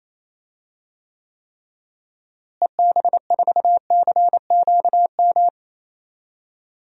Morse code, heard as a keyed tone, about 28 wpm, 700 Hz. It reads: EB4CQM
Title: DXCC-100 Call Signs - 6 Characters - Repeated 3x - 28wpm